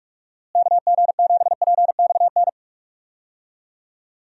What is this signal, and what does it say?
Morse code, heard as a keyed tone, about 46 wpm, 700 Hz. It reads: KG7PXN
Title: USA Call Signs - 6 Characters - Repeated 2x - 45wpm